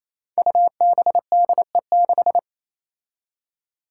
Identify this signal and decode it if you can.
Morse code, heard as a keyed tone, about 28 wpm, 700 Hz. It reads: UBDE6
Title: Five Challenging Letter-Number Combinations 28wpm